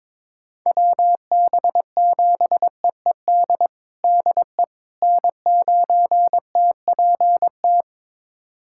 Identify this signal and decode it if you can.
Morse code, heard as a keyed tone, about 22 wpm, 700 Hz. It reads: WB7EED DE N9TPT
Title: USA Callsign DE Callsign 22wpm